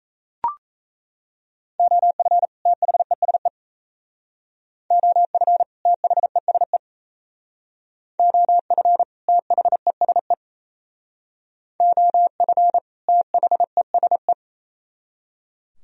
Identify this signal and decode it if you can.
Morse code, heard as a keyed tone, about 38 wpm, 700 Hz. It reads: OF THESE OF THESE OF THESE OF THESE